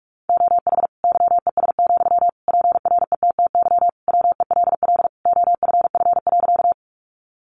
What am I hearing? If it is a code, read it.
Morse code, heard as a keyed tone, about 44 wpm, 700 Hz. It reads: OH YES, PRETTY WELL OFF.
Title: Sentences from Top 1000 Words 45wpm